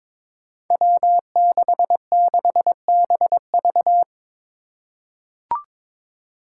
Morse code, heard as a keyed tone, about 22 wpm, 700 Hz. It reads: W66BV
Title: Five Challenging Letter-Number Combinations 22wpm